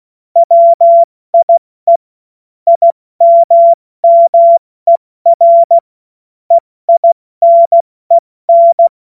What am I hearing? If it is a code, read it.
Morse code, heard as a keyed tone, about 16 wpm, spaced as for 14 wpm, 680 Hz. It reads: WIE IMMER EINEN
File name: Morsekurs / Witze / Witz 31. Mai 2022 70 F80